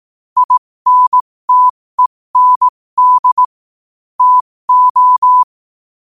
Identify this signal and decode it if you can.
Morse code, heard as a keyed tone, about 18 wpm, spaced as for 15 wpm, 1000 Hz. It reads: INTEND TO
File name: qotdfiles / 15wpm